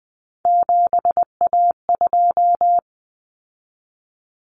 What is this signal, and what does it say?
Morse code, heard as a keyed tone, about 20 wpm, 700 Hz. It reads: 7A2
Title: Common Cut Numbers - 3-Digit 20wpm